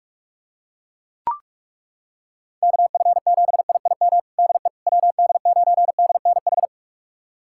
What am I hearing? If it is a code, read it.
Morse code, heard as a keyed tone, about 41 wpm, 700 Hz. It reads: KF7IIM DE WD9DNH
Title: USA Callsign DE Callsign - 6 Characters - 45wpm